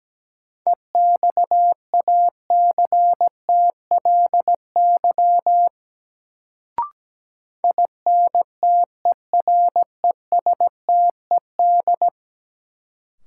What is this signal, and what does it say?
Morse code, heard as a keyed tone, about 17 wpm, 700 Hz. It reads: EXACTLY INTERESTED
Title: Sentences from Top 900 Words - New Word Drill - 17wpm